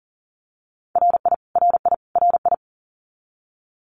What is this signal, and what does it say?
Morse code, heard as a keyed tone, about 40 wpm, 700 Hz. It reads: RI RI RI
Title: US States Repeated 3x 40wpm